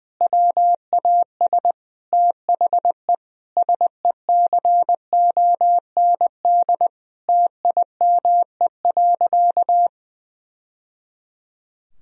Morse code, heard as a keyed tone, about 21 wpm, 700 Hz. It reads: WAS THE SECOND TIME.
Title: Sentences from Top 1000 Words - Review Part 1 - 20wpm